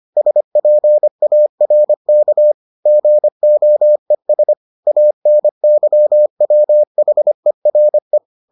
Morse code, heard as a keyed tone, about 25 wpm, 600 Hz. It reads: SPARK GOES ANYWHERE